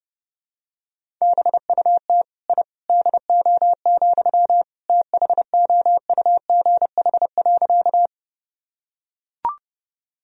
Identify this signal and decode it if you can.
Morse code, heard as a keyed tone, about 30 wpm, 700 Hz. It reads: BUT I DO, THOUGH.
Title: Sentences from Top 1000 Words - Review Part 2 - 30wpm